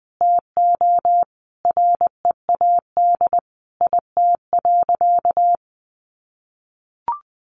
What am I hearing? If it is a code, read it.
Morse code, heard as a keyed tone, about 20 wpm, 700 Hz. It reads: TO READ IT.
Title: Sentences from Top 500 Words 20wpm